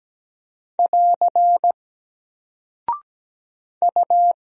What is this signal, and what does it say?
Morse code, heard as a keyed tone, about 17 wpm, 700 Hz. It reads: <AR> U
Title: Single Character (Letter:Number:Punctuation:Prosign) 17wpm